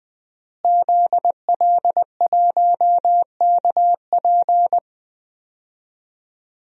Morse code, heard as a keyed tone, about 20 wpm, 700 Hz. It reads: ZL1KP
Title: DXCC-100 Call Signs - 5 Characters - Repeated 3x - 20wpm